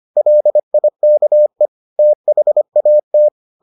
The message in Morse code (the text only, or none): LIKE THAT